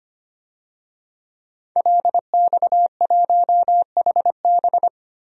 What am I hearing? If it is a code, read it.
Morse code, heard as a keyed tone, about 25 wpm, 700 Hz. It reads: LX1HB
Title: DXCC-100 Call Signs - 5 Characters - 25wpm